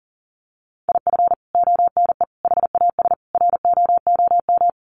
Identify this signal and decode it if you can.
Morse code, heard as a keyed tone, about 40 wpm, 700 Hz. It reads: IF ONE HAS ROOM
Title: Sentences from Top 1400 Words 40wpm